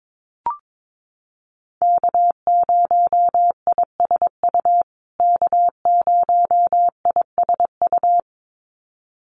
Morse code, heard as a keyed tone, about 22 wpm, 700 Hz. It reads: K0ISU K0ISU
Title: USA Call Signs Repeated 2x 22wpm